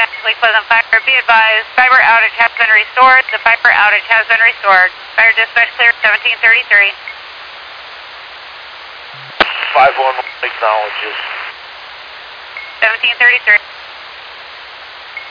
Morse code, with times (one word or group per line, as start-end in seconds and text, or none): none